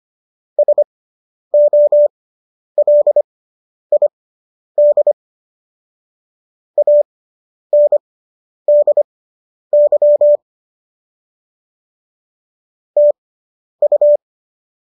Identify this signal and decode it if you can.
Morse code, heard as a keyed tone, about 25 wpm, spaced as for 10 wpm, 590 Hz. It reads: SOLID ANDY TU